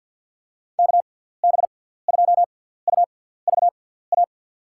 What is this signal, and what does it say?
Morse code, heard as a keyed tone, about 53 wpm, spaced as for 17 wpm, 700 Hz. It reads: KB2UVA
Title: USA Call Signs - 6 Characters - 6x Character Spacing - 50wpm